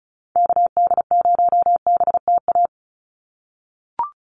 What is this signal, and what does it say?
Morse code, heard as a keyed tone, about 34 wpm, 700 Hz. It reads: KD0BTA
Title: USA Call Signs 35wpm